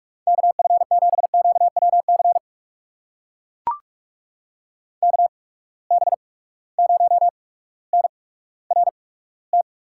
Morse code, heard as a keyed tone, about 43 wpm, spaced as for 19 wpm, 700 Hz. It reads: KF7QWC KB0NRT
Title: USA Call Signs - 6 Characters - 8x Character Spacing - 45wpm